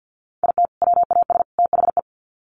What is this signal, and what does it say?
Morse code, heard as a keyed tone, about 50 wpm, 700 Hz. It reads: IT WAS THE
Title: Sets of 3 Words 50wpm